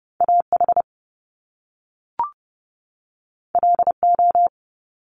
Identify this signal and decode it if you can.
Morse code, heard as a keyed tone, about 30 wpm, 700 Hz. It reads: AH LO
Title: Top Two Letter Combinations in Words 30wpm